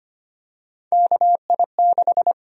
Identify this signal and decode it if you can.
Morse code, heard as a keyed tone, about 25 wpm, 700 Hz. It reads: KI6
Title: USA Call Sign Prefixes - Repeated 2x - 25wpm